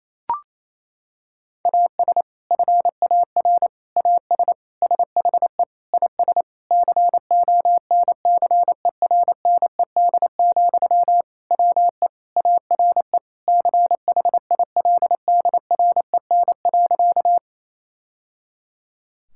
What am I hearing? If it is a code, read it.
Morse code, heard as a keyed tone, about 28 wpm, 700 Hz. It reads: AS FAR AS SHE IS CONCERNED, WE ARE CHILDREN.
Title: Sentences from Top 1000 Words - Review Part 6 - 28wpm